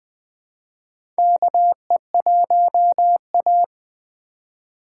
Morse code, heard as a keyed tone, about 20 wpm, 700 Hz. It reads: KE1A
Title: USA Call Signs - 4 Characters - 2x Character Spacing - 20wpm